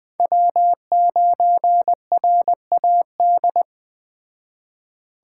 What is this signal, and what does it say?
Morse code, heard as a keyed tone, about 21 wpm, 700 Hz. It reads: W9RAD
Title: Speed Racing - USA 1x3 Call Signs 17wpm